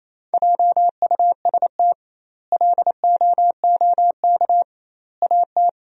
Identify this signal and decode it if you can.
Morse code, heard as a keyed tone, about 28 wpm, 700 Hz. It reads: JUST LOOK AT
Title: 2x Word-Spacing - Sentences from Top 200 Words - 28wpm